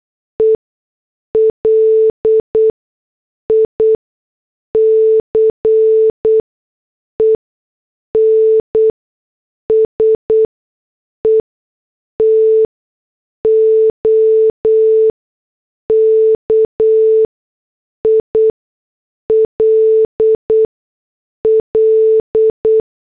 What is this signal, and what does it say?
Morse code, heard as a keyed tone, about 8 wpm, spaced as for 6 wpm, 440 Hz. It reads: E LICENSE TO KILL
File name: morse / headlines